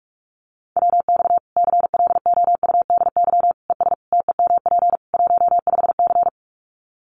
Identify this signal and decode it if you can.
Morse code, heard as a keyed tone, about 44 wpm, 700 Hz. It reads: WX CLOUDY ES TEMP 15C
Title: Common QSO Phrases 45wpm